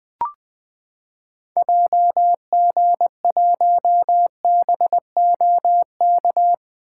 Morse code, heard as a keyed tone, about 20 wpm, 700 Hz. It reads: JG1BOK
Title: 2023 SOTA Activator Call Signs - 20wpm